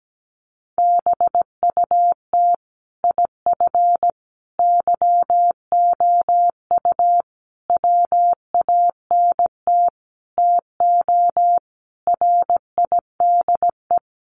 Morse code, heard as a keyed tone, about 17 wpm, 700 Hz. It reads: BUT IF YOU WANT TO RIDE
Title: Sentences from Top 1200 Words 17wpm